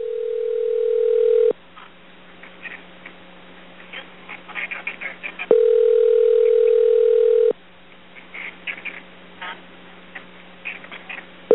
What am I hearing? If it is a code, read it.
Ringback tone (North America), 440 + 480 Hz.